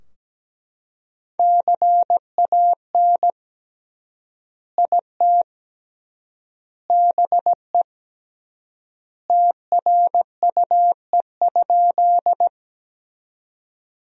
Morse code, heard as a keyed tone, about 17 wpm, 700 Hz. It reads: CAN IT BE TRUE?
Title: 3x Word-Spacing - Sentences from Top 400 Words - 17wpm